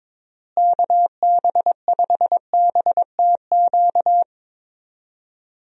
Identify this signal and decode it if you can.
Morse code, heard as a keyed tone, about 22 wpm, 700 Hz. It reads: KB5BTQ
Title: USA Call Signs - 6 Characters - Repeated 3x - 22wpm